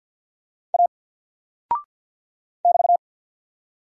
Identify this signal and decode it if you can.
Morse code, heard as a keyed tone, about 48 wpm, 700 Hz. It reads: A <BT>
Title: Single Character (Letter:Number:Punctuation:Prosign) 50wpm